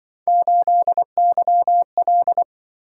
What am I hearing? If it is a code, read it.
Morse code, heard as a keyed tone, about 25 wpm, 700 Hz. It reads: 8YL